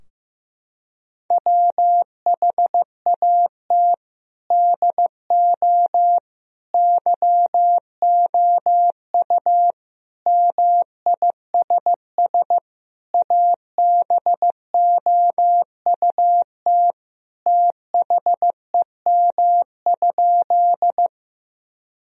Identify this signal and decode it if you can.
Morse code, heard as a keyed tone, about 16 wpm, 700 Hz. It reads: WHAT DO YOU MISS ABOUT THEM?
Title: Sentences from Top 1000 Words - Review Part 3 - 15wpm